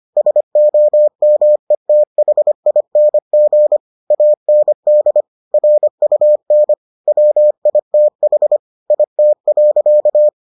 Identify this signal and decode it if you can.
Morse code, heard as a keyed tone, about 25 wpm, 600 Hz. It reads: SOMETHING AND RUN WITH IT.